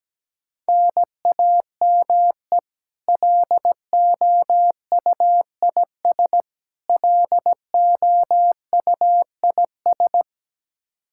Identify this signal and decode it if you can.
Morse code, heard as a keyed tone, about 17 wpm, 700 Hz. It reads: NAME LOUIS LOUIS
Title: K1USN SST QSO Practice 17wpm